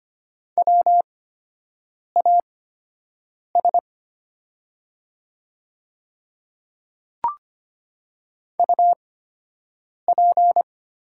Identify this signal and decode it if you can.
Morse code, heard as a keyed tone, about 25 wpm, spaced as for 7 wpm, 700 Hz. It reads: WAS UP
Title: 8x Character Spacing - Top 100 Words - 25wpm